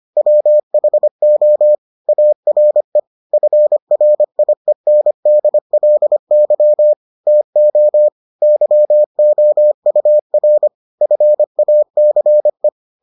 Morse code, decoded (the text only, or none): WHO ARE FRIENDLY TO YOUR FACE